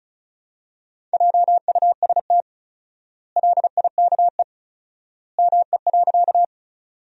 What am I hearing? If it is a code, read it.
Morse code, heard as a keyed tone, about 36 wpm, 700 Hz. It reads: JUST LIKE ME.
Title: 4x Word-Spacing - Sentences from Top 200 Words - 35wpm